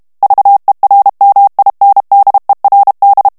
Morse code, heard as a keyed tone, about 32 wpm, 800 Hz. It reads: VERMINDERD
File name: morse mp3 / HOLLANDAIS texte clair 2 vitesse 30 mots minute